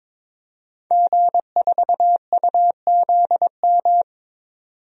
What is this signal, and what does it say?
Morse code, heard as a keyed tone, about 22 wpm, 700 Hz. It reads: G4UZM
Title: DXCC-100 Call Signs - 5 Characters - 22wpm